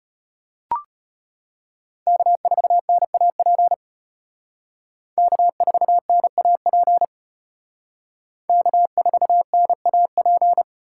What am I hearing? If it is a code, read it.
Morse code, heard as a keyed tone, about 33 wpm, 700 Hz. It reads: K4NAP K4NAP K4NAP